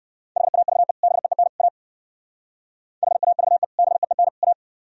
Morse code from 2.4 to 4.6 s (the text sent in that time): HAVE BEEN A